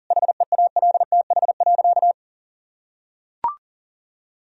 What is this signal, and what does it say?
Morse code, heard as a keyed tone, about 40 wpm, 700 Hz. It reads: HEALTH.
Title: Sentences from Top 1000 Words - Review Part 3 - 40wpm